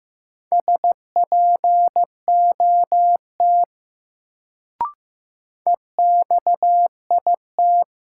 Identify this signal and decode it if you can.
Morse code, heard as a keyed tone, about 15 wpm, 700 Hz. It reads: SPOT EXIT
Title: Sentences from Top 1100 Words - New Word Drill - 15wpm